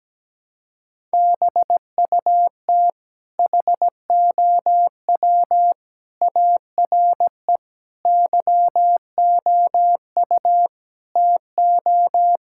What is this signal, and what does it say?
Morse code, heard as a keyed tone, about 17 wpm, 700 Hz. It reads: BUT HOW ARE YOU TO
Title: Sentences from Top 1000 Words - Review Part 6 - 17wpm